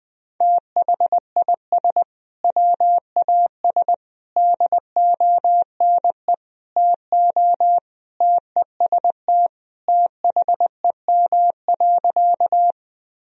Morse code, heard as a keyed tone, about 20 wpm, 700 Hz. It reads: THIS WAS DONE TO TEST THEM.
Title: Sentences from Top 1000 Words - Review Part 3 - 20wpm